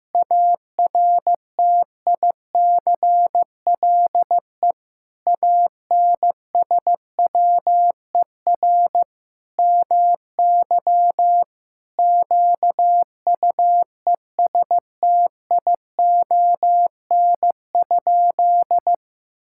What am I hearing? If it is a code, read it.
Morse code, heard as a keyed tone, about 16 wpm, 700 Hz. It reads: ARTICLE ANSWER MY QUESTION?
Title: Sentences from Top 1000 Words - Review Part 5 - 15wpm